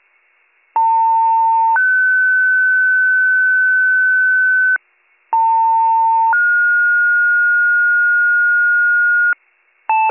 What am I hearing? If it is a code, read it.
Two-tone page, heard as a one-second tone, then a three-second tone, about 900 then 1500 Hz, about 900 then 1430 Hz. It is heard from 0.8 to 9.3 s.